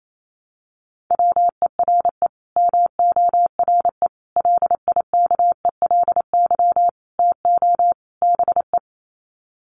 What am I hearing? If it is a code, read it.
Morse code, heard as a keyed tone, about 28 wpm, 700 Hz. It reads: WERE MORE LIKELY TO BE
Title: Sets of 5 Words Encore2 28wpm